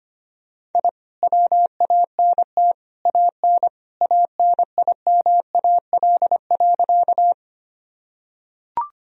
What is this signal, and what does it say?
Morse code, heard as a keyed tone, about 25 wpm, 700 Hz. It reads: I WANT AN ANIMAL.
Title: Sentences from Top 1000 Words - Review Part 1 - 25wpm